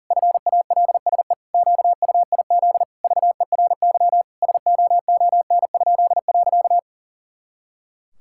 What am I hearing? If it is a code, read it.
Morse code, heard as a keyed tone, about 40 wpm, 700 Hz. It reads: FALSE QUIZ VERY SOON?.
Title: Pangram Challenge v2 - 40wpm